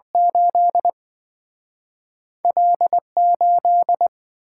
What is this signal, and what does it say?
Morse code, heard as a keyed tone, about 27 wpm, spaced as for 24 wpm, 700 Hz. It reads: 8 L8
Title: Speed-Racing - Two Letter-Number Combinations - 20wpm